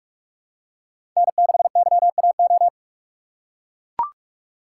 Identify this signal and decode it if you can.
Morse code, heard as a keyed tone, about 44 wpm, 700 Hz. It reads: N6YAO
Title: USA Call Signs - 5 Characters - 3x Character Spacing - 45wpm